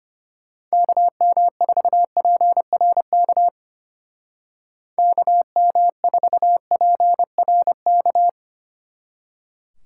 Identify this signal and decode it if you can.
Morse code, heard as a keyed tone, about 31 wpm, 700 Hz. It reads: KM4PRK KM4PRK
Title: Speed Racing - USA Call Signs 25wpm